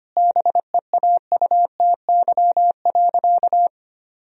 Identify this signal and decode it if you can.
Morse code, heard as a keyed tone, about 25 wpm, 700 Hz. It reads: BEAUTY.
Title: Sentences from Top 1000 Words - Review Part 2 - 25wpm